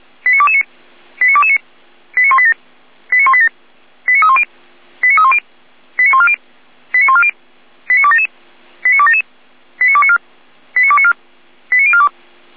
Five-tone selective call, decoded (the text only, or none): 89229 89229 89187 89187 89310 89310 89140 89140 89277 89277 89294 89294 89042